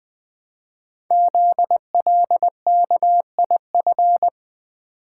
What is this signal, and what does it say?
Morse code, heard as a keyed tone, about 20 wpm, 700 Hz. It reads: ZLKIF
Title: Code Groups - 5 Letters - 20wpm